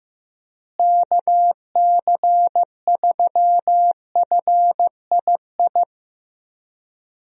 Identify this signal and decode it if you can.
Morse code, heard as a keyed tone, about 15 wpm, 700 Hz. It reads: KC3FII
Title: USA Call Signs - 6 Characters - Repeated 2x - 15wpm